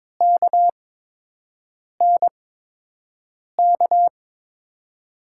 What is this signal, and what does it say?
Morse code, heard as a keyed tone, about 22 wpm, spaced as for 6 wpm, 700 Hz. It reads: KNK
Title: USA Call Signs - 6 Characters - 8x Character Spacing - 22wpm